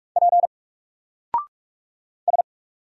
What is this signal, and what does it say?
Morse code, heard as a keyed tone, about 44 wpm, 700 Hz. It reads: P S
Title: Single Character (Letter:Number:Punctuation:Prosign) 45wpm